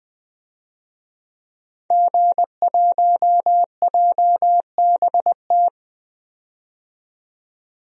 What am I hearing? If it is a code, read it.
Morse code, heard as a keyed tone, about 21 wpm, 700 Hz. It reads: G1JBT
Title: DXCC-100 Call Signs - 5 Characters - 20wpm